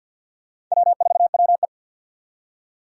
Morse code, heard as a keyed tone, about 51 wpm, 700 Hz. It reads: W4WE
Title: Four Challenging Letter-Number Combinations 50wpm